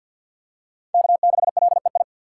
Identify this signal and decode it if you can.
Morse code, heard as a keyed tone, about 52 wpm, 700 Hz. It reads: K6LEI
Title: USA Call Signs - 2x Character Spacing - 50wpm